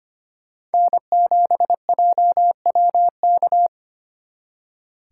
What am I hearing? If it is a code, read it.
Morse code, heard as a keyed tone, about 31 wpm, 700 Hz. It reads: N7JWK
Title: Speed Racing - USA 1x3 Call Signs 25wpm